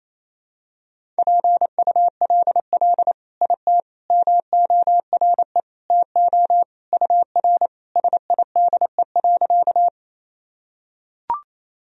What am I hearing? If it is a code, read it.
Morse code, heard as a keyed tone, about 28 wpm, 700 Hz. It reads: PULL IT MORE TO UR SIDE.
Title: Sentences from Top 400 Words and Abbreviations 28wpm